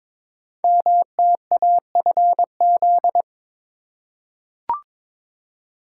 Morse code, heard as a keyed tone, about 22 wpm, 700 Hz. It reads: MTAFZ
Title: Code Groups - 5 Letters - 22wpm